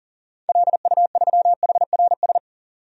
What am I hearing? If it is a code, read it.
Morse code, heard as a keyed tone, about 40 wpm, 700 Hz. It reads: LU3HRS
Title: DXCC-100 Call Signs - 6 Characters - 40wpm